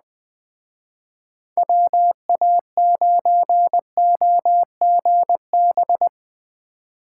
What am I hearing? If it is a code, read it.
Morse code, heard as a keyed tone, about 19 wpm, 700 Hz. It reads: WA9OGB